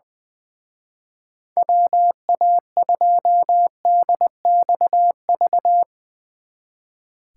Morse code, heard as a keyed tone, about 20 wpm, 700 Hz. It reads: WA2DXV
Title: Speed Racing - USA 2x3 Call Signs 20wpm